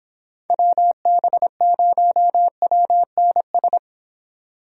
Morse code, heard as a keyed tone, about 26 wpm, 700 Hz. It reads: WB0WNS